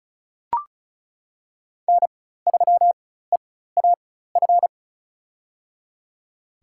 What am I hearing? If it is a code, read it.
Morse code, heard as a keyed tone, about 36 wpm, spaced as for 16 wpm, 700 Hz. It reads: N3EAF
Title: USA Call Signs - 5 Characters - 4x Character Spacing - 35wpm